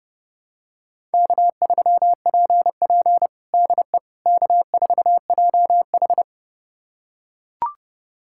Morse code, heard as a keyed tone, about 32 wpm, 700 Hz. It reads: K3PP DE K4JH
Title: USA Callsign DE Callsign - 4 Characters - 30wpm